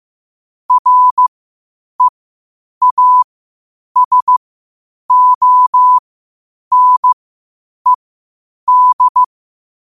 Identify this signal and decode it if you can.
Morse code, heard as a keyed tone, about 15 wpm, spaced as for 8 wpm, 1000 Hz. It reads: REASONED
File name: qotdfiles / 8wpm